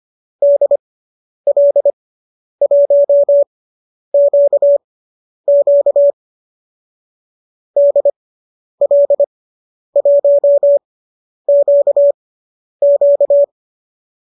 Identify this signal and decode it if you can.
Morse code, heard as a keyed tone, about 25 wpm, spaced as for 10 wpm, 590 Hz. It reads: DL1QQ DL1QQ
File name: qso219 10